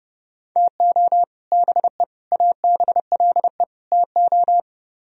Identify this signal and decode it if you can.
Morse code, heard as a keyed tone, about 38 wpm, 700 Hz. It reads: TO BE ABLE TO